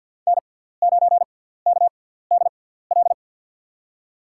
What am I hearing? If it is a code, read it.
Morse code, heard as a keyed tone, about 53 wpm, spaced as for 17 wpm, 700 Hz. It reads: N9KDL
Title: USA Call Signs - 5 Characters - 6x Character Spacing - 50wpm